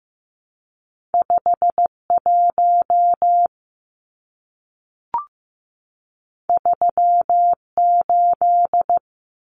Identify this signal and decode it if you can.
Morse code, heard as a keyed tone, about 15 wpm, 700 Hz. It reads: 51 38
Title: Numbers - 2-Digit 15wpm